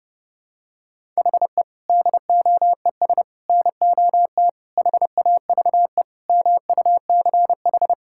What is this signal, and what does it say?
Morse code, heard as a keyed tone, about 32 wpm, 700 Hz. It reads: HE DOES NOT HAVE MUCH
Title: Sentences from Top 1000 Words - Review Part 5 - 30wpm